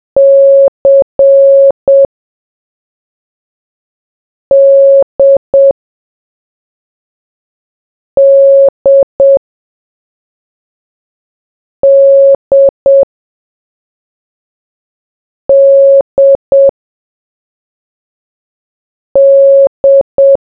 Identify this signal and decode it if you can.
Morse code, heard as a keyed tone, about 7 wpm, 550 Hz. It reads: C D D D D D